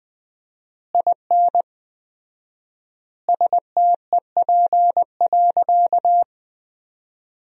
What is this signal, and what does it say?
Morse code, heard as a keyed tone, about 21 wpm, 700 Hz. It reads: IN STEP.
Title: 4x Word-Spacing - Sentences from Top 400 Words - 20wpm